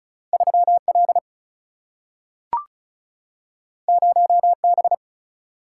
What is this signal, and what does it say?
Morse code, heard as a keyed tone, about 36 wpm, 700 Hz. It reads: 3 L 0 B
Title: Single-Letter Single-Number 35wpm